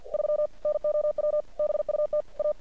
Morse code, heard as a keyed tone, about 50 wpm, 600 Hz. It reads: 3 NOW BUT A
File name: archived / CRSnet